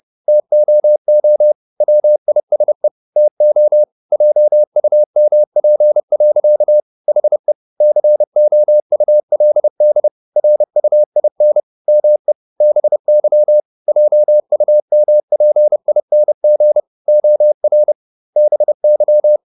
Morse code, heard as a keyed tone, about 30 wpm, 600 Hz. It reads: TOO WISE TO JUMP. HE COULD RUIN ME BY JUMPING OR BY
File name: the old man and the sea / the old man and the sea 1 30wpm